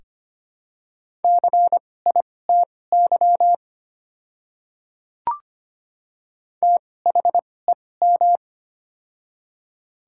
Morse code, heard as a keyed tone, about 25 wpm, spaced as for 18 wpm, 700 Hz. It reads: CITY THEM
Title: 2x Character Spacing - Top 500 Words - 25wpm